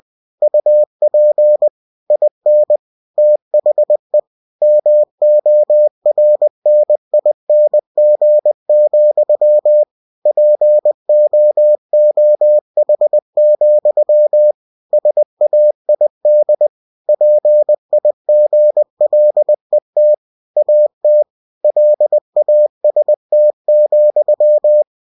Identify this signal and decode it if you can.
Morse code, heard as a keyed tone, about 20 wpm, 600 Hz. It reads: UP IN THE MORNING, POOH, SAID PIGLET AT LAST,
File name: cw-books / winnie-the-pooh / winnie-20wpm-page-0110